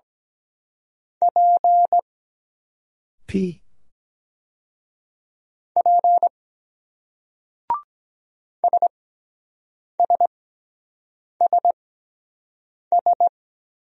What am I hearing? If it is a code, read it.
Morse code, heard as a keyed tone, about 23 wpm, spaced as for 7 wpm, 700 Hz. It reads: P P SSSS